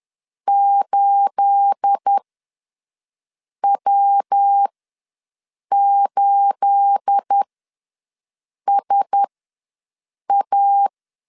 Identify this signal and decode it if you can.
Morse code, heard as a keyed tone, about 10 wpm, spaced as for 5 wpm, 800 Hz. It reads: 8W8SA
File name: LEZIONI / 9 28 ws 10wpm 4124